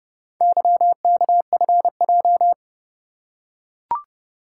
Morse code, heard as a keyed tone, about 29 wpm, 700 Hz. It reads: YKFJ